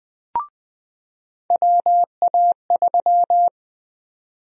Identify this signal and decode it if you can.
Morse code, heard as a keyed tone, about 20 wpm, 700 Hz. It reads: WA3
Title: USA Call Sign Prefixes - 20wpm